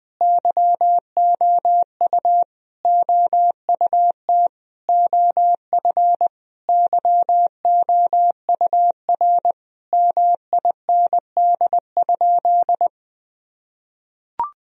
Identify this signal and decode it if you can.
Morse code, heard as a keyed tone, about 21 wpm, 700 Hz. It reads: YOU OUT OF YOUR MIND?
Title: Sentences from Top 1000 Words - Review Part 2 - 20wpm